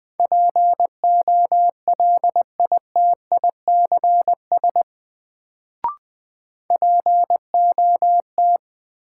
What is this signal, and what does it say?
Morse code, heard as a keyed tone, about 20 wpm, 700 Hz. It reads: POLITICS POT
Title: Sentences from Top 1000 Words - New Word Drill - 20wpm